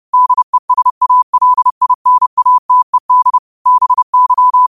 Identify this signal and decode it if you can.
Morse code, heard as a keyed tone, about 30 wpm, 1000 Hz. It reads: DESALINATED BY
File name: qotdfiles / 30wpm